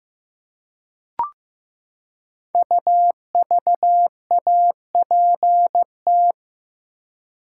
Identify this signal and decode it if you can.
Morse code, heard as a keyed tone, about 15 wpm, 700 Hz. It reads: UVAPT
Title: Code Groups - 5 Letter:Numbers - 15wpm